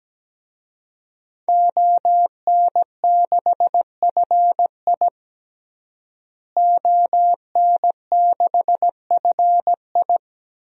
Morse code, heard as a keyed tone, about 17 wpm, 700 Hz. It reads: ON6FI ON6FI
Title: DXCC-100 Call Signs - 5 Characters - Repeated 3x - 17wpm